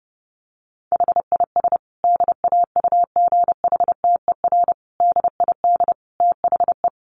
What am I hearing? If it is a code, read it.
Morse code, heard as a keyed tone, about 30 wpm, 700 Hz. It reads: HIS DAUGHTER DID THE
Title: Sentences from Top 1500 Words 30wpm